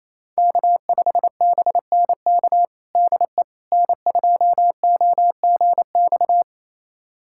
Morse code, heard as a keyed tone, about 28 wpm, 700 Hz. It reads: K5BNK DE N2OGX
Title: USA Callsign DE Callsign - 5 Characters - 28wpm